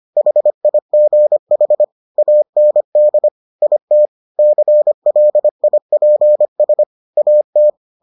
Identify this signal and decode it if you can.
Morse code, heard as a keyed tone, about 25 wpm, 600 Hz. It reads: HIGH AND IT CLIPS AT